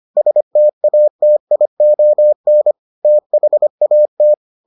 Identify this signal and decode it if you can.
Morse code, heard as a keyed tone, about 25 wpm, 600 Hz. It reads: STATION THAT